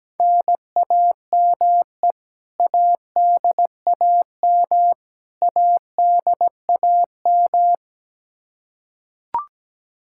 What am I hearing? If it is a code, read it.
Morse code, heard as a keyed tone, about 17 wpm, 700 Hz. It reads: NAME ADAM ADAM
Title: K1USN SST QSO Practice 17wpm